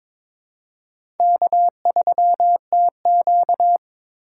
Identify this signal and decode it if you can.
Morse code, heard as a keyed tone, about 22 wpm, 700 Hz. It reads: K3TQ
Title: USA Call Signs - 4 Characters - 2x Character Spacing - 22wpm